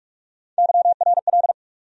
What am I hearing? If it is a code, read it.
Morse code, heard as a keyed tone, about 46 wpm, 700 Hz. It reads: YRL
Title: USA Call Sign Postfixes - 3 Letters 45wpm